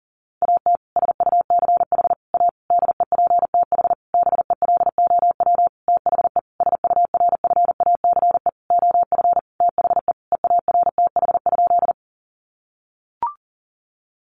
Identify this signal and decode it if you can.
Morse code, heard as a keyed tone, about 40 wpm, 700 Hz. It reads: AT SUCH A DEPTH BELOW THE SURFACE OF THE EARTH?
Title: Sentences from Top 1200 Words 40wpm